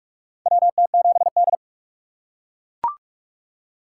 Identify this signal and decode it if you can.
Morse code, heard as a keyed tone, about 41 wpm, 700 Hz. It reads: WT7D
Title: USA Call Signs - 4 Characters - 4x Character Spacing - 45wpm